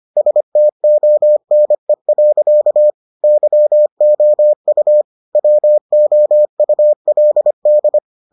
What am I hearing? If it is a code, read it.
Morse code, heard as a keyed tone, about 25 wpm, 600 Hz. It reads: STONE. YOU WOULD